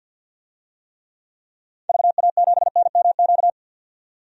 Morse code, heard as a keyed tone, about 48 wpm, 700 Hz. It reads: VA7NMQ
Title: DXCC-100 Call Signs - 6 Characters - 50wpm